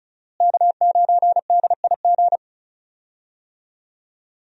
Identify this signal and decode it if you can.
Morse code, heard as a keyed tone, about 36 wpm, 700 Hz. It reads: K9DIG
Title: USA Call Signs - 5 Characters - Repeated 3x - 35wpm